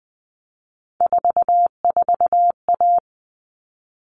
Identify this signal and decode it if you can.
Morse code, heard as a keyed tone, about 20 wpm, 700 Hz. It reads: 44A
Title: All Cut Numbers - 3-Digit 20wpm